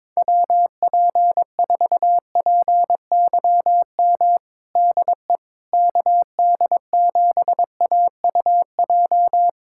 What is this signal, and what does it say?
Morse code, heard as a keyed tone, about 22 wpm, 700 Hz. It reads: WP4PYM DE KD7AUJ
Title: USA Callsign DE Callsign - 6 Characters - 22wpm